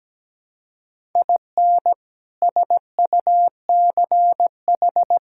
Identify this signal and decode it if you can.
Morse code, heard as a keyed tone, about 17 wpm, 700 Hz. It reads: IN SUCH
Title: Sentences from Top 1000 Words - Review Part 5 - 17wpm